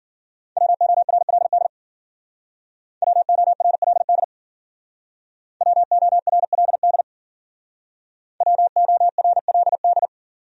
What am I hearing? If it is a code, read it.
Morse code, heard as a keyed tone, about 52 wpm, 700 Hz. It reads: WORLD WORLD WORLD WORLD